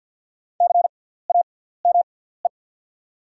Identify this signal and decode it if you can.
Morse code, heard as a keyed tone, about 48 wpm, spaced as for 17 wpm, 690 Hz. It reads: CAME
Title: 6x Character Spacing - Top 200 Words - 50wpm